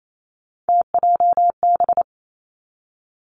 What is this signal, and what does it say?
Morse code, heard as a keyed tone, about 28 wpm, 690 Hz. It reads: TJB
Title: Three Challenging Letter-Number Combinations 28wpm